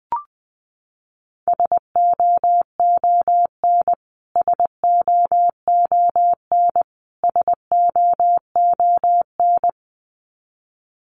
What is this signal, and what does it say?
Morse code, heard as a keyed tone, about 20 wpm, 700 Hz. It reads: SOON SOON SOON
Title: Top 500 Words Repeated 20wpm